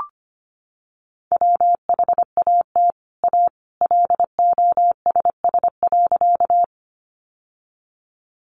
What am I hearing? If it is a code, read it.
Morse code, heard as a keyed tone, about 25 wpm, 700 Hz. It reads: WHAT A LOSS.
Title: Sentences from Top 1400 Words 25wpm